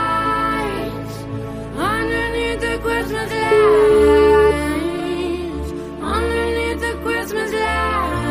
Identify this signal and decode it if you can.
Ringback tone (Europe, 425 Hz), with noise.